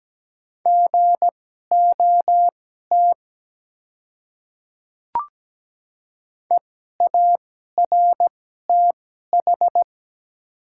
Morse code, heard as a keyed tone, about 17 wpm, spaced as for 12 wpm, 690 Hz. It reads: GOT EARTH
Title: 2x Character Spacing - Top 300 Words - 17wpm